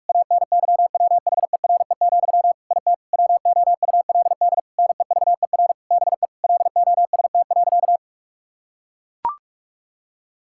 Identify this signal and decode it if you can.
Morse code, heard as a keyed tone, about 43 wpm, 700 Hz. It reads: ANYWHERE, IT WOULD NEVER BE LOST.
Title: Sentences from Top 1600 Words 45wpm